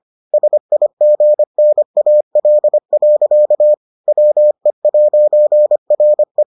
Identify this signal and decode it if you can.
Morse code, heard as a keyed tone, about 25 wpm, 600 Hz. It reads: SIGNAL. WE'RE